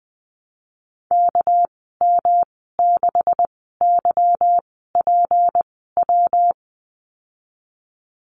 Morse code, heard as a keyed tone, about 20 wpm, spaced as for 15 wpm, 700 Hz. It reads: KM6YPW
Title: 200 Lesson 32g - Letter k - Call Signs 20wpm